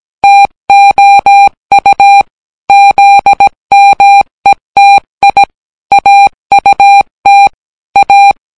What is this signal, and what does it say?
Morse code, heard as a keyed tone, about 17 wpm, 800 Hz. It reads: TOU ZMETI AUT A